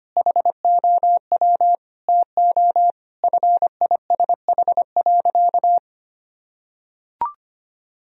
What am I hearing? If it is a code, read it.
Morse code, heard as a keyed tone, about 25 wpm, 700 Hz. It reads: HOW TO FISH.
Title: Sentences from Top 1000 Words - Review Part 1 - 25wpm